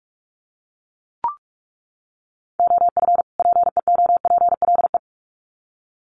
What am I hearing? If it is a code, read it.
Morse code, heard as a keyed tone, about 44 wpm, 700 Hz. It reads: OF PEOPLE
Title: Sets of 2 Words 45wpm